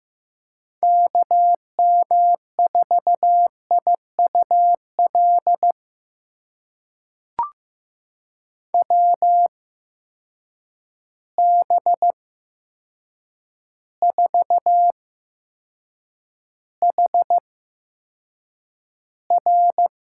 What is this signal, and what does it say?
Morse code, heard as a keyed tone, about 15 wpm, spaced as for 7 wpm, 700 Hz. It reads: KM4IUL WB4HR
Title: USA Call Signs - 6 Characters - 8x Character Spacing - 15wpm